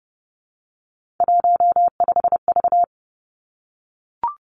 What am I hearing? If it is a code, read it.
Morse code, heard as a keyed tone, about 30 wpm, 700 Hz. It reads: 15V
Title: Three Challenging Letter-Number Combinations 30wpm